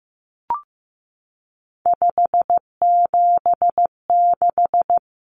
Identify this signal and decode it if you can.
Morse code, heard as a keyed tone, about 15 wpm, 700 Hz. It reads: 576
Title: Numbers - 3-Digit 15wpm